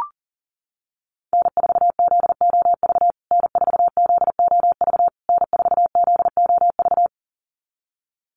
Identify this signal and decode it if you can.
Morse code, heard as a keyed tone, about 37 wpm, 700 Hz. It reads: N4ZOV N4ZOV N4ZOV
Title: USA Call Signs Repeated 3x 40wpm